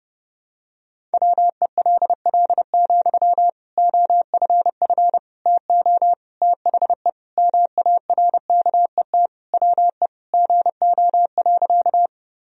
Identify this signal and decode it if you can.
Morse code, heard as a keyed tone, about 30 wpm, 700 Hz. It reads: WELL, OFF TO THE MARKET WE GO.
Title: Sentences from Top 1000 Words - Review Part 3 - 30wpm